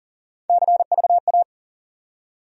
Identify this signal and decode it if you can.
Morse code, heard as a keyed tone, about 40 wpm, 700 Hz. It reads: CVA
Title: USA Call Sign Postfixes - 3 Letters 40wpm